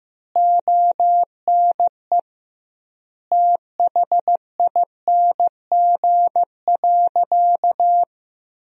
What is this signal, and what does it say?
Morse code, heard as a keyed tone, about 15 wpm, 700 Hz. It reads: ONE THING.
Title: 2x Word-Spacing - Sentences from Top 300 Words - 15wpm